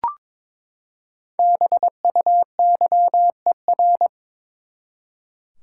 Morse code, heard as a keyed tone, about 22 wpm, 700 Hz. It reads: BUYER
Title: Five Letter Words - 22wpm